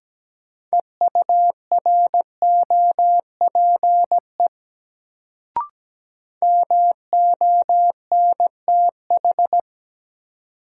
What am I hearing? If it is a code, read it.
Morse code, heard as a keyed tone, about 17 wpm, 700 Hz. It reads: EUROPE MONTH
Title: Sets of 5 Words Encore - New Word Drill - 17wpm